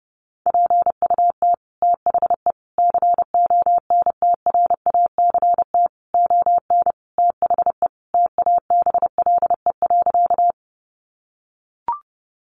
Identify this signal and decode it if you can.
Morse code, heard as a keyed tone, about 30 wpm, 700 Hz. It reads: PUT THE CONTRACT ON THE TABLE.
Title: Sentences from Top 1400 Words 30wpm